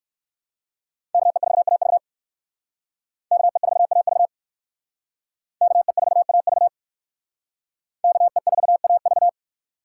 Morse code, heard as a keyed tone, about 67 wpm, 700 Hz. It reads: KE4AV KE4AV KE4AV KE4AV